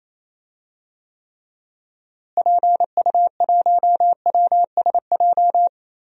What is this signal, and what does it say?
Morse code, heard as a keyed tone, about 28 wpm, 700 Hz. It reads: PU1WSJ
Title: DXCC-100 Call Signs - 6 Characters - Repeated 3x - 28wpm